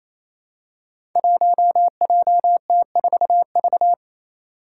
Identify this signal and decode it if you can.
Morse code, heard as a keyed tone, about 27 wpm, 700 Hz. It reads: 1JT4V
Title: Five Challenging Letter-Number Combinations 28wpm